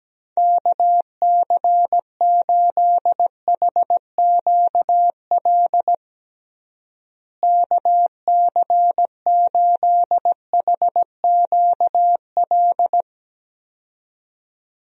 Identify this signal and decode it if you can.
Morse code, heard as a keyed tone, about 17 wpm, 700 Hz. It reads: KC8HQL KC8HQL
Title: USA Call Signs - 6 Characters - Repeated 2x - 17wpm